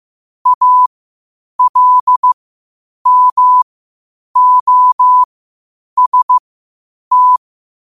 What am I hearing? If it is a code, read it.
Morse code, heard as a keyed tone, about 15 wpm, spaced as for 8 wpm, 1000 Hz. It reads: ALMOST
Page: New quotes every day in morse code at 8 Words per minute.